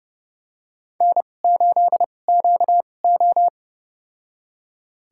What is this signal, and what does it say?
Morse code, heard as a keyed tone, about 30 wpm, spaced as for 22 wpm, 700 Hz. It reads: N8QO
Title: USA Call Signs - 4 Characters - 2x Character Spacing - 30wpm